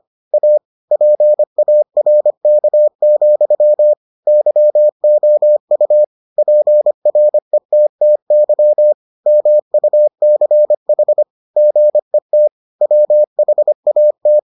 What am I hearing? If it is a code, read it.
Morse code, heard as a keyed tone, about 25 wpm, 600 Hz. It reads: A PARK, YOU PRETTY MUCH GET WHAT